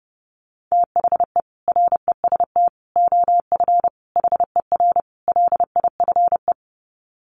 Morse code, heard as a keyed tone, about 30 wpm, 700 Hz. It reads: THE REST OF HER LIFE
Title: Sets of 5 Words Encore 30wpm